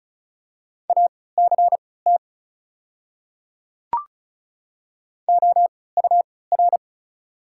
Morse code, heard as a keyed tone, about 36 wpm, spaced as for 20 wpm, 690 Hz. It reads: ACT OUR
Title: 3x Character Spacing - Top 300 Words - 35wpm